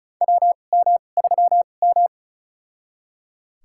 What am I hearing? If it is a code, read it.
Morse code, heard as a keyed tone, about 34 wpm, spaced as for 25 wpm, 690 Hz. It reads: WM3M
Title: USA Call Signs - 4 Characters - 2x Character Spacing - 35wpm